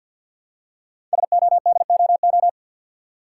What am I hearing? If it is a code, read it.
Morse code, heard as a keyed tone, about 50 wpm, 700 Hz. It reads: SODOO
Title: Code Groups - 5 Letters - 50wpm